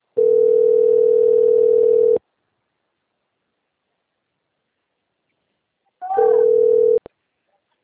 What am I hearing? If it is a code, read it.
Ringback tone (North America), 440 + 480 Hz.
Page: Limeira IV